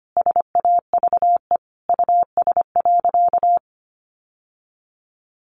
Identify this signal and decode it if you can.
Morse code, heard as a keyed tone, about 25 wpm, 700 Hz. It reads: SAVE US.
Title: Sentences from Top 1100 Words 25wpm